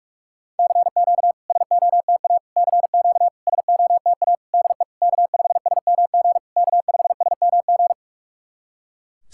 Morse code, heard as a keyed tone, about 44 wpm, 700 Hz. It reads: CQ SOTA CQ SOTA DE K5SMG K5SMG
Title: SOTA QSO Practice 45wpm